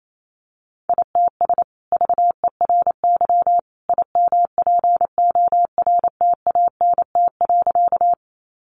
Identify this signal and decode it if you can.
Morse code, heard as a keyed tone, about 28 wpm, 700 Hz. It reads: ITS VERY IMPORTANT.
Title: Sentences from Top 600 Words 28wpm